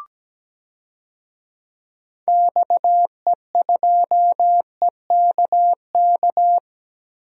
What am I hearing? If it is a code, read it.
Morse code, heard as a keyed tone, about 17 wpm, 700 Hz. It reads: XE2EKK
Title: DXCC-100 Call Signs - 6 Characters - Repeated 2x - 17wpm